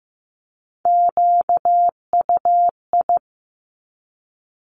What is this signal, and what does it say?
Morse code, heard as a keyed tone, about 15 wpm, 700 Hz. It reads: QUI
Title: Top Three Letter Combinations in Words 15wpm